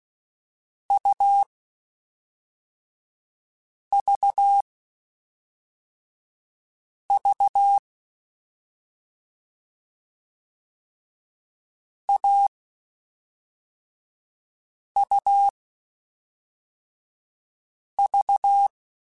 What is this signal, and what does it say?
Morse code, heard as a keyed tone, about 16 wpm, 800 Hz. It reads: U V V A U V
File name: Short Runs / Short Runs-mp3s / AUV AUV4 @5